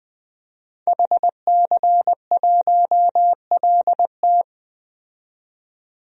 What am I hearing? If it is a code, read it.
Morse code, heard as a keyed tone, about 20 wpm, 700 Hz. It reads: HC1LT
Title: DXCC-100 Call Signs - 5 Characters - 20wpm